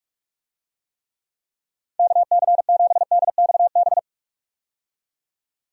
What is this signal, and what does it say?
Morse code, heard as a keyed tone, about 46 wpm, 700 Hz. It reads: KC7DXB
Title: USA Call Signs - 6 Characters - 45wpm